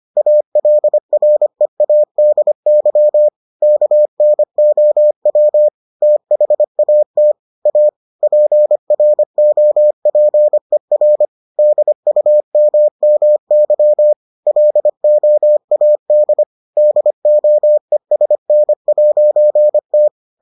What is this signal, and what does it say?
Morse code, heard as a keyed tone, about 25 wpm, 600 Hz. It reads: ALREADY KNOW THAT A PROPER DUMMY LOAD DOESN'T